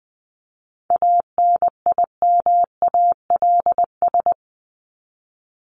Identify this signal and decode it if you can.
Morse code, heard as a keyed tone, about 20 wpm, 700 Hz. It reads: ANIMALS
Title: ES:S Suffix - Top 100 Words with Base Words 20wpm